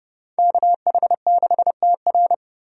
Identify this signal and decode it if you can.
Morse code, heard as a keyed tone, about 30 wpm, 700 Hz. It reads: KH6TR
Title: DXCC-100 Call Signs - 5 Characters - 30wpm